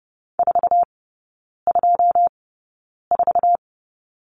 Morse code, heard as a keyed tone, about 30 wpm, spaced as for 9 wpm, 700 Hz. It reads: 424
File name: 081 Lesson 14e - Number 2 - Three Characters 30wpm